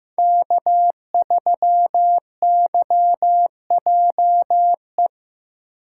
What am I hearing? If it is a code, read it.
Morse code, heard as a keyed tone, about 15 wpm, 700 Hz. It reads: K3YJE
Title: USA Call Signs - 5 Characters - 3x Character Spacing - 15wpm